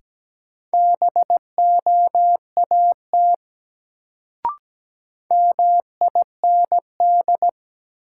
Morse code, heard as a keyed tone, about 17 wpm, 700 Hz. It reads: BOAT MIND
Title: Sentences from Top 500 Words - New Word Drill - 17wpm